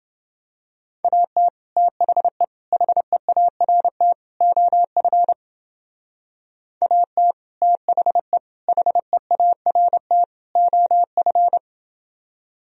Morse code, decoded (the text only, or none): AT THE HEART OF AT THE HEART OF